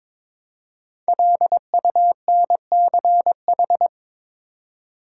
Morse code, heard as a keyed tone, about 22 wpm, 700 Hz. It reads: LUNCH
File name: 5 Letter Words - First 3 Letters Given - 22wpm